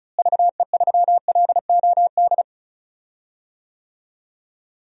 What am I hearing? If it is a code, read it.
Morse code, heard as a keyed tone, about 34 wpm, 700 Hz. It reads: VE3LOD
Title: DXCC-100 Call Signs - 6 Characters - Repeated 2x - 35wpm